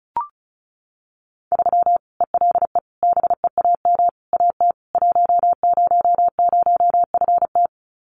Morse code, heard as a keyed tone, about 34 wpm, 700 Hz. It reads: 3 ELE BEAM AT 100FT
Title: Common QSO Phrases 35wpm